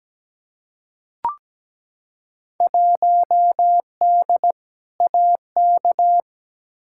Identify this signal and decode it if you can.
Morse code, heard as a keyed tone, about 17 wpm, 700 Hz. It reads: 1D AK
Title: Field Day Exchanges - 17wpm